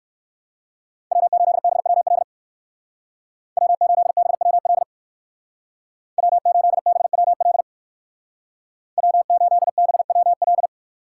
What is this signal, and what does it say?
Morse code, heard as a keyed tone, about 59 wpm, 700 Hz. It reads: W8BWL W8BWL W8BWL W8BWL